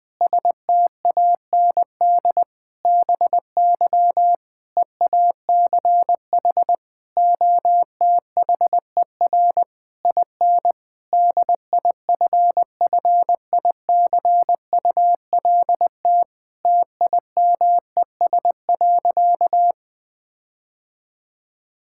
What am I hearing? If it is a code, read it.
Morse code, heard as a keyed tone, about 20 wpm, 700 Hz. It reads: STAND BY EACH OTHER IN DIFFICULT TIMES.
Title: Difficult Sentences with Hints - 20wpm